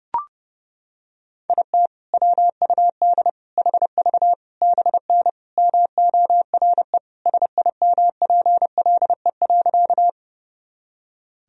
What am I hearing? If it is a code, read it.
Morse code, heard as a keyed tone, about 32 wpm, 700 Hz. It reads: IT WUD HV BN MORE SIMPLE.
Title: Sentences from Top 400 Words and Abbreviations 30wpm